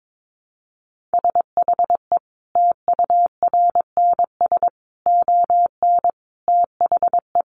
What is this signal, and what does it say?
Morse code, heard as a keyed tone, about 22 wpm, 700 Hz. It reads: SHE TURNS ON THE
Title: Easy Phrases 22wpm